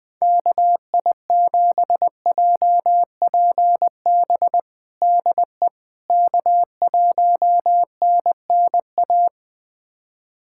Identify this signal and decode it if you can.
Morse code, heard as a keyed tone, about 20 wpm, 700 Hz. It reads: KI7JPB DE K1NNA
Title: USA Callsign DE Callsign 20wpm